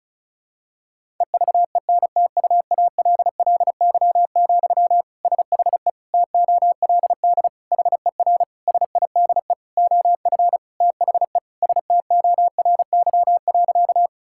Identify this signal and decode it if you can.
Morse code, heard as a keyed tone, about 34 wpm, 700 Hz. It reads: EVENTUALLY, SHE TOLD HER SIDE OF THE STORY.
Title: Sentences from Top 1000 Words - Review Part 6 - 35wpm